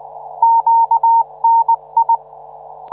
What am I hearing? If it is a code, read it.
Morse code, heard as a keyed tone, about 19 wpm, 920 Hz. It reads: QNI